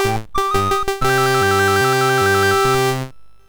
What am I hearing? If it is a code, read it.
There is no code.